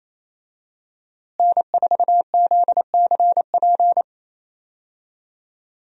Morse code, heard as a keyed tone, about 27 wpm, 700 Hz. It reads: N4ZCP
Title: USA Call Signs - 5 Characters - Repeated 2x - 28wpm